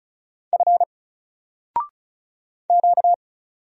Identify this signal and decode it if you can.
Morse code, heard as a keyed tone, about 35 wpm, 700 Hz. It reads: F Q
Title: Single Letter-Number 35wpm